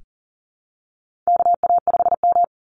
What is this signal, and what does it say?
Morse code, heard as a keyed tone, about 40 wpm, 700 Hz. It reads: KA5M
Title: Call Signs Repeated 3x - 40wpm